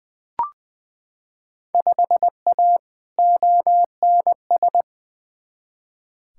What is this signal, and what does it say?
Morse code, heard as a keyed tone, about 20 wpm, 700 Hz. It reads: 5A ONS
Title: Field Day Exchanges Encore - 20wpm